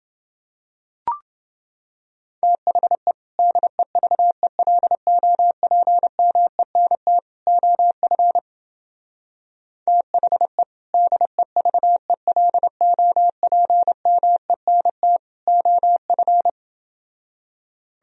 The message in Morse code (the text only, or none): THE DEVELOPMENT OF THE DEVELOPMENT OF